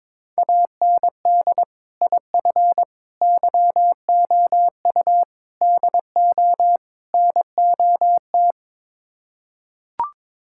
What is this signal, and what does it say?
Morse code, heard as a keyed tone, about 22 wpm, 700 Hz. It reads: AND IF YOU DO NOT
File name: practice / 22wpm / Build A Phrase - 5-Word Phrases - 22wpm